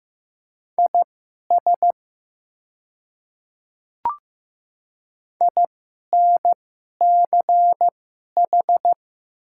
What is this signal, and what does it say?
Morse code, heard as a keyed tone, about 15 wpm, spaced as for 11 wpm, 700 Hz. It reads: IS INCH
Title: 2x Character Spacing - Top 500 Words - 15wpm